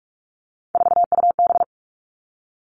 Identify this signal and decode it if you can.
Morse code, heard as a keyed tone, about 44 wpm, 700 Hz. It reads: 4UB
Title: All Cut Numbers - 3-Digit 45wpm